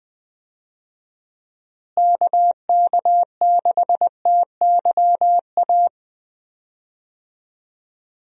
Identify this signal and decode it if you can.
Morse code, heard as a keyed tone, about 21 wpm, 700 Hz. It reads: KK6TYA
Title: USA Call Signs - 6 Characters - Repeated 2x - 20wpm